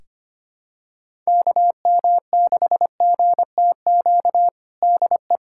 Morse code, heard as a keyed tone, about 25 wpm, 700 Hz. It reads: KM6GTQ DE
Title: USA Callsign DE Callsign - 6 Characters - 25wpm